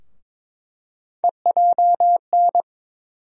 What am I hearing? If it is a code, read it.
Morse code, heard as a keyed tone, about 22 wpm, 700 Hz. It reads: EJN
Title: USA Call Sign Postfixes - 3 Letters 22wpm